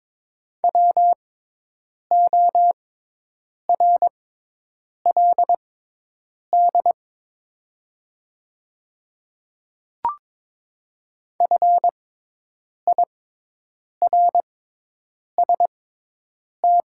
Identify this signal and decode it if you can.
Morse code, heard as a keyed tone, about 22 wpm, spaced as for 8 wpm, 700 Hz. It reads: WORLD FIRST
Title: 6x Character Spacing - Top 200 Words - 22wpm